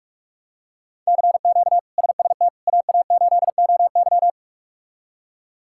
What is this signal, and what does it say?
Morse code, heard as a keyed tone, about 46 wpm, 700 Hz. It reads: CQ SST AA8OY
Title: K1USN SST QSO Practice - 45wpm